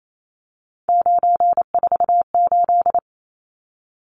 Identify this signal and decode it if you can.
Morse code, heard as a keyed tone, about 28 wpm, 700 Hz. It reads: 948
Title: Numbers - 3-Digit 28wpm